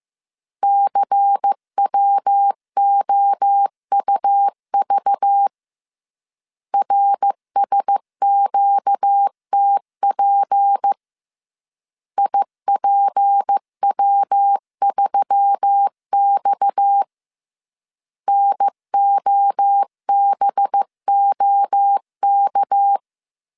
Morse code, heard as a keyed tone, about 15 wpm, 800 Hz. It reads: CWOUV RSQTP IPW3X NOBOK